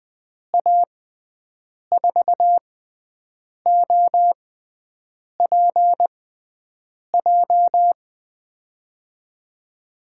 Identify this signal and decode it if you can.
Morse code, heard as a keyed tone, about 20 wpm, spaced as for 7 wpm, 700 Hz. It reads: A4OPJ
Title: USA Call Signs - 6 Characters - 6x Character Spacing - 20wpm